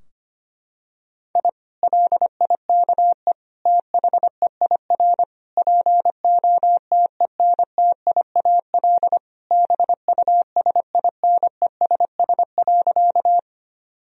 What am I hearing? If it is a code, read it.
Morse code, heard as a keyed tone, about 25 wpm, 700 Hz. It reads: I LIKE THEIR POTENTIAL BUSINESS.
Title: Sentences from Top 1000 Words - Review Part 4 - 25wpm